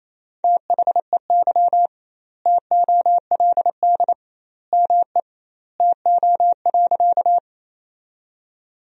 Morse code, heard as a keyed tone, about 28 wpm, 700 Hz. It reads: THEY TOLD ME TO.
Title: 2x Word-Spacing - Sentences from Top 400 Words - 28wpm